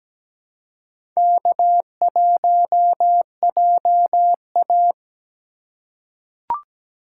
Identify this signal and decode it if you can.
Morse code, heard as a keyed tone, about 17 wpm, 700 Hz. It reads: K1JA
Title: USA Call Signs - 4 Characters - 3x Character Spacing - 17wpm